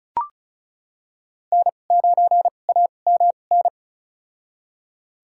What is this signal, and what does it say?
Morse code, heard as a keyed tone, about 34 wpm, spaced as for 25 wpm, 700 Hz. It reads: N9AMN
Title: USA Call Signs - 5 Characters - 2x Character Spacing - 35wpm